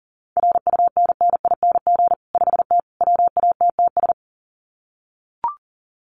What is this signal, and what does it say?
Morse code, heard as a keyed tone, about 40 wpm, 700 Hz. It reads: RUNNING 5T WATTS
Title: Common QSO Phrases 40wpm